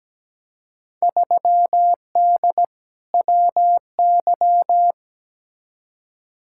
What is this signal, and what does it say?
Morse code, heard as a keyed tone, about 17 wpm, 700 Hz. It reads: 3D WY
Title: Field Day Exchanges Encore - 17wpm